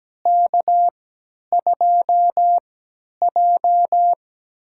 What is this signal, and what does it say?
Morse code, heard as a keyed tone, about 17 wpm, spaced as for 10 wpm, 700 Hz. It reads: K2J
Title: USA Call Signs - 3x Character Spacing - 17wpm